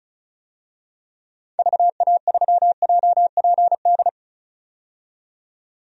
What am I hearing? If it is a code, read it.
Morse code, heard as a keyed tone, about 34 wpm, 700 Hz. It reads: VA3JPD
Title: DXCC-100 Call Signs - 6 Characters - 35wpm